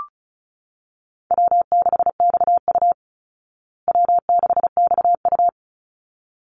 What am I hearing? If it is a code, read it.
Morse code, heard as a keyed tone, about 34 wpm, 700 Hz. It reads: W6XU W6XU
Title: Easy Call Signs Repeated 2x - 35wpm